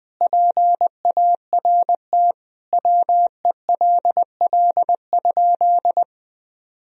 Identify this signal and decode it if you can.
Morse code, heard as a keyed tone, about 20 wpm, 700 Hz. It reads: PART WELL?
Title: Sentences from Top 1000 Words - Review Part 1 - 20wpm